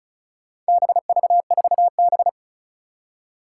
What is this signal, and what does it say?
Morse code, heard as a keyed tone, about 36 wpm, 700 Hz. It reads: BV4B
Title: Four Challenging Letter-Number Combinations 35wpm